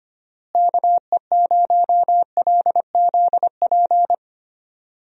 Morse code, heard as a keyed tone, about 25 wpm, 700 Hz. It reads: KE0LZP
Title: USA Call Signs - 6 Characters - Repeated 3x - 25wpm